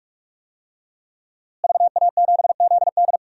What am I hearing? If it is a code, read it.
Morse code, heard as a keyed tone, about 41 wpm, 700 Hz. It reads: VA7ZD
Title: DXCC-100 Call Signs - 5 Characters - Repeated 3x - 45wpm